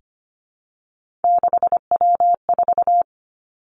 Morse code, heard as a keyed tone, about 25 wpm, 700 Hz. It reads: 6W4
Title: Three Challenging Letter-Number Combinations 25wpm